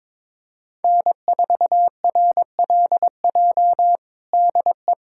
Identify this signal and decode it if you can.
Morse code, heard as a keyed tone, about 22 wpm, 700 Hz. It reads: N4RLJ DE
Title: USA Callsign DE Callsign - 5 Characters - 22wpm